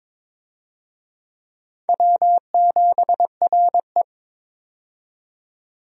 Morse code, heard as a keyed tone, about 21 wpm, 700 Hz. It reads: W7RE
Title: USA Call Signs - 4 Characters - 22wpm